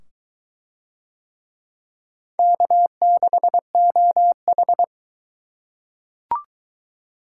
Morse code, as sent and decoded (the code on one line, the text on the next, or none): -.- -.... --- ....
K6OH